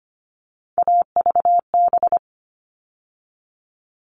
Morse code, heard as a keyed tone, about 25 wpm, 700 Hz. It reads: AVB
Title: All Cut Numbers - 3-Digit 25wpm